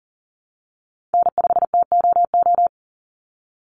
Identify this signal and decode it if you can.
Morse code, heard as a keyed tone, about 40 wpm, 700 Hz. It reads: N5TOO
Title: Call Signs Repeated 3x - 40wpm